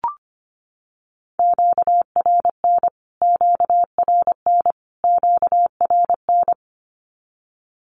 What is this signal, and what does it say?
Morse code, heard as a keyed tone, about 25 wpm, 700 Hz. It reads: QRN QRN QRN
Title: Q-Codes Repeated 3x 25wpm